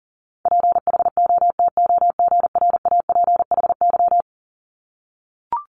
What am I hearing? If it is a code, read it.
Morse code, heard as a keyed tone, about 39 wpm, 700 Hz. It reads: PHOTOGRAPHY
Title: Multi-letter Phonemes Repeated 3x 40wpm